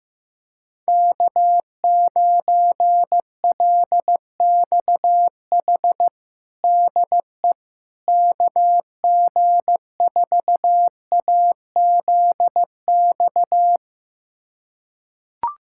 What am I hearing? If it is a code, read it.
Morse code, heard as a keyed tone, about 15 wpm, 700 Hz. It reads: K9LXH DE KG4AZX
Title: USA Callsign DE Callsign 15wpm